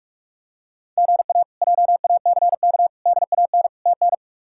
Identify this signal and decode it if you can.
Morse code, heard as a keyed tone, about 45 wpm, 700 Hz. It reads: GA JACK DAN TN
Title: K1USN SST QSO Practice - 45wpm